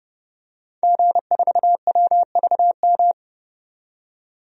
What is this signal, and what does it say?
Morse code, heard as a keyed tone, about 31 wpm, 700 Hz. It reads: G4WVM
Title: DXCC-100 Call Signs - 5 Characters - 30wpm